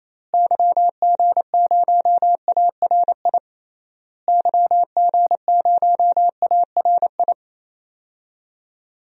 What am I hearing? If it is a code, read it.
Morse code, heard as a keyed tone, about 28 wpm, 700 Hz. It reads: YG0ARI YG0ARI
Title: DXCC-100 Call Signs - 6 Characters - Repeated 3x - 28wpm